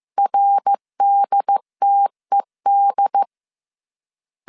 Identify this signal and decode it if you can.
Morse code, heard as a keyed tone, about 15 wpm, 800 Hz. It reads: RDTED